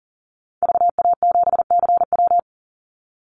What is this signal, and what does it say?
Morse code, heard as a keyed tone, about 40 wpm, 700 Hz. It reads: VA7CW
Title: Call Signs - 40wpm